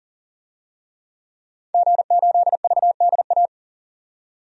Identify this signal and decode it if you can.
Morse code, heard as a keyed tone, about 39 wpm, 700 Hz. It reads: G8VDA
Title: DXCC-100 Call Signs - 5 Characters - 40wpm